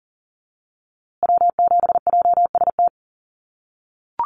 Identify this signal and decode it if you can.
Morse code, heard as a keyed tone, about 40 wpm, 700 Hz. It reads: W7JST
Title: USA Call Signs Repeated 2x 40wpm